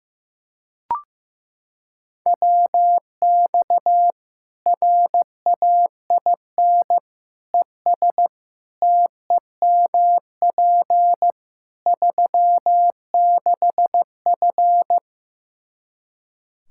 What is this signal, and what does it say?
Morse code, heard as a keyed tone, about 15 wpm, 700 Hz. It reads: WX RAIN ES TEMP 36F
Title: Practice QSO - Exchange 2 - Info - 15wpm